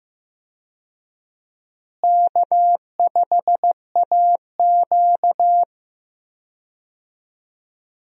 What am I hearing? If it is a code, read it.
Morse code, heard as a keyed tone, about 15 wpm, 700 Hz. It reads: K5AQ
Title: USA Call Signs - 4 Characters - Repeated 2x - 15wpm